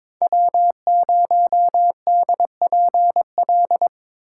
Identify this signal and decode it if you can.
Morse code, heard as a keyed tone, about 23 wpm, 700 Hz. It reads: W0DPL
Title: USA Call Signs - 5 Characters - 3x Character Spacing - 22wpm